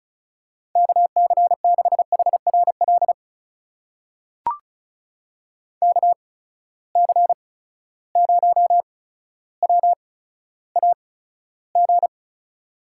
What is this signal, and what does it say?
Morse code, heard as a keyed tone, about 36 wpm, spaced as for 15 wpm, 700 Hz. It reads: KC6HRL KC0WAG
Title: USA Call Signs - 6 Characters - 8x Character Spacing - 35wpm